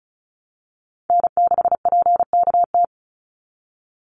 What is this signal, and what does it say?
Morse code, heard as a keyed tone, about 34 wpm, 700 Hz. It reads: N6PKT
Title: NA SOTA Chaser Callsigns 35wpm